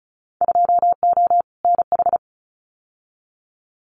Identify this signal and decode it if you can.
Morse code, heard as a keyed tone, about 34 wpm, 700 Hz. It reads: 2O NH
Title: Winter Field Day Exchanges 35wpm